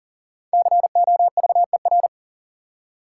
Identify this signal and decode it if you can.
Morse code, heard as a keyed tone, about 40 wpm, 700 Hz. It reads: COVER
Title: Five Letter Words - 40wpm